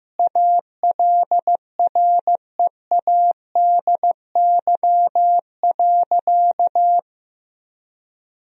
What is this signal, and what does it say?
Morse code, heard as a keyed tone, about 15 wpm, 700 Hz. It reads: ALREADY.
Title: Sentences from Top 1000 Words - Review Part 2 - 15wpm